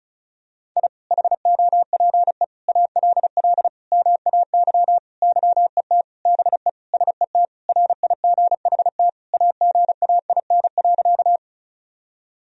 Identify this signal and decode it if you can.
Morse code, heard as a keyed tone, about 34 wpm, 690 Hz. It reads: I HOPE ALL MAY YET BE SET RIGHT AGAIN.
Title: Sentences from Top 1000 Words - Review Part 1 - 35wpm